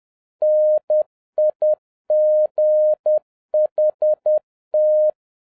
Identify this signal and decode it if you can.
Morse code, heard as a keyed tone, about 10 wpm, 610 Hz. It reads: NIGHT
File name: 2011-07-mp3 / n56 BACK FROM A BIG ACCIDENT 10wpm